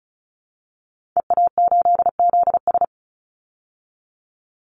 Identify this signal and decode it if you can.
Morse code, heard as a keyed tone, about 34 wpm, 700 Hz. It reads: EA8ZS
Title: Call Signs Repeated 3x - 35wpm